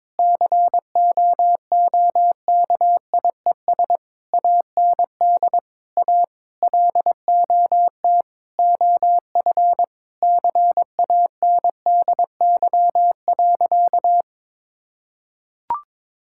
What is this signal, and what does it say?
Morse code, heard as a keyed tone, about 22 wpm, 700 Hz. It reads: COOKIES AND A LOT OF CANDY.
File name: practice / 22wpm / Sentences with 4-Word Phrases - 22wpm